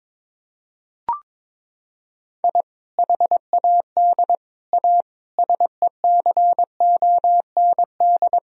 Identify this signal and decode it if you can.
Morse code, heard as a keyed tone, about 22 wpm, 700 Hz. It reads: I HAD A SECOND
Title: Sentences from Top 1000 Words - Review Part 1 - 22wpm